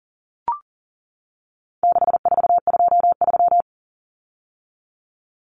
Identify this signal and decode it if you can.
Morse code, heard as a keyed tone, about 40 wpm, 700 Hz. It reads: 6423
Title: Numbers - 4-Digit 40wpm